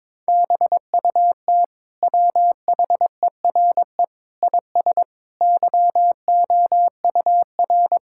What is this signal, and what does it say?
Morse code, heard as a keyed tone, about 22 wpm, 700 Hz. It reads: BUT WHERE IS YOUR
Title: Sentences from Top 1000 Words - Review Part 5 - 22wpm